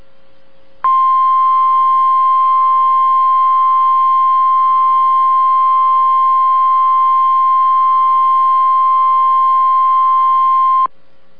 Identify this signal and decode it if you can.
Emergency Alert System (SAME) tones (the 1050 Hz attention tone). It is heard from 0.8 to 10.9 s.